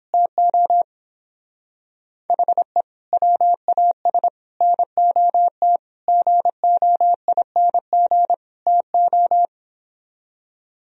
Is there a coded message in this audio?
Yes: Morse code from 0.1 to 9.5 s, heard as a keyed tone, about 29 wpm, 700 Hz. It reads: TO HE WAS NOT GOING TO